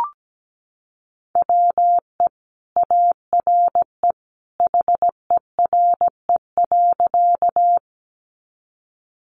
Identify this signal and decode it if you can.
Morse code, heard as a keyed tone, about 17 wpm, 700 Hz. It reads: WE ARE HERE.
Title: Sentences from Top 200 Words 17wpm